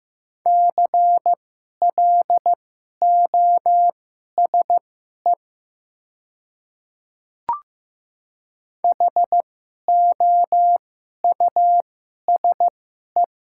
Morse code, heard as a keyed tone, about 16 wpm, spaced as for 11 wpm, 700 Hz. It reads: CLOSE HOUSE
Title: 2x Character Spacing - Top 300 Words - 15wpm